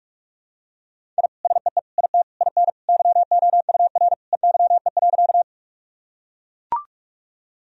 Morse code, heard as a keyed tone, about 41 wpm, 700 Hz. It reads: I SEE IT IN YOUR EYE.
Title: Sentences from Top 1000 Words - Review Part 1 - 45wpm